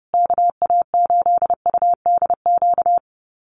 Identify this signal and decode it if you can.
Morse code, heard as a keyed tone, about 28 wpm, 700 Hz. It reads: KA8UDQ
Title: USA Call Signs Repeated 3x 30wpm